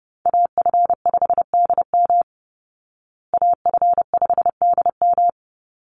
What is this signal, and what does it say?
Morse code, heard as a keyed tone, about 30 wpm, 700 Hz. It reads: AF5DM AF5DM
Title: Call Signs Repeated 2x - 30wpm